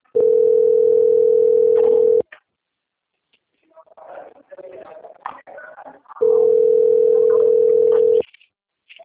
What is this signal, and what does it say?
Ringback tone (North America), 440 + 480 Hz.
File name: out- - - - - .44419